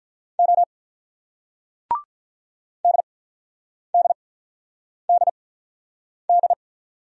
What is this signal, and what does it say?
Morse code, heard as a keyed tone, about 40 wpm, spaced as for 8 wpm, 700 Hz. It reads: O DDDD
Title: Speed-Racing Single Letters 35wpm